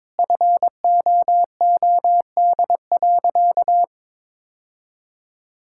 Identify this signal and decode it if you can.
Morse code, heard as a keyed tone, about 21 wpm, 700 Hz. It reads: FOOD.
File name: practice / 22wpm / Sentences with 3-Word Phrases - 22wpm